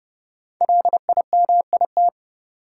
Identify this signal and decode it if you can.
Morse code, heard as a keyed tone, about 30 wpm, 700 Hz. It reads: LIMIT
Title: Sentences from Top 1500 Words - New Word Drill Alt - 30wpm